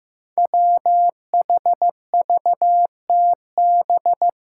Morse code, heard as a keyed tone, about 15 wpm, 700 Hz. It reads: WHVTB
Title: Five Challenging Letter-Number Combinations 15wpm